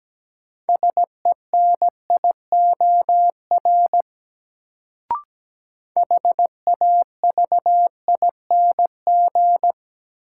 Morse code, heard as a keyed tone, about 17 wpm, 700 Hz. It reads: SENIOR HAVING
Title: Sets of 5 Words Encore - New Word Drill - 17wpm